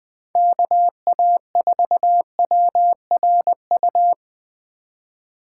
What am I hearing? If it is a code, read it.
Morse code, heard as a keyed tone, about 20 wpm, 700 Hz. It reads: KA4WRU
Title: USA Call Signs - 6 Characters - 2x Character Spacing - 20wpm